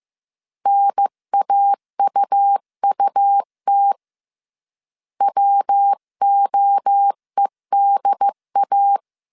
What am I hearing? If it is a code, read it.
Morse code, heard as a keyed tone, about 15 wpm, 800 Hz. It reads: NAUUT WOEDA
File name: CW / DET / 1 3 1500C S C 14wpm 2458